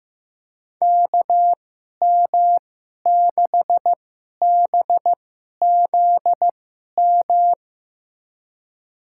Morse code, heard as a keyed tone, about 15 wpm, spaced as for 11 wpm, 700 Hz. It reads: KM6BZM
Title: USA Call Signs - 6 Characters - 2x Character Spacing - 15wpm